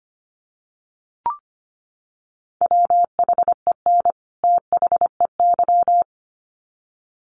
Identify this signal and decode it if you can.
Morse code, heard as a keyed tone, about 25 wpm, 700 Hz. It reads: WHEN THEY
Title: Sets of 2 Words 25wpm